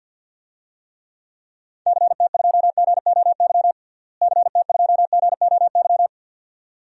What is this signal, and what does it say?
Morse code, heard as a keyed tone, about 48 wpm, 700 Hz. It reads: CT2GOY CT2GOY
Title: DXCC-100 Call Signs - 6 Characters - Repeated 2x - 50wpm